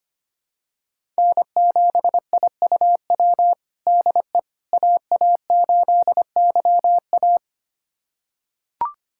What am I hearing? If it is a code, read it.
Morse code, heard as a keyed tone, about 25 wpm, 700 Hz. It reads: N7IUW DE AA8YA
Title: USA Callsign DE Callsign - 5 Characters - 25wpm